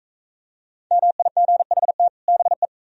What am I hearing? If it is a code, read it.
Morse code, heard as a keyed tone, about 42 wpm, 700 Hz. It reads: MIGHT BE